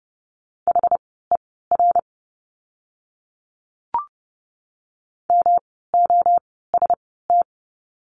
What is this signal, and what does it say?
Morse code, heard as a keyed tone, about 30 wpm, spaced as for 17 wpm, 700 Hz. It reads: HER MOST
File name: course / 30:10wpm / 230 Lesson 37f - Letter x - Words 30wpm